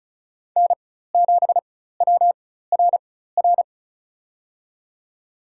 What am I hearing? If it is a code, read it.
Morse code, heard as a keyed tone, about 34 wpm, spaced as for 16 wpm, 700 Hz. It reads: N7WRR
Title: USA Call Signs - 5 Characters - 4x Character Spacing - 35wpm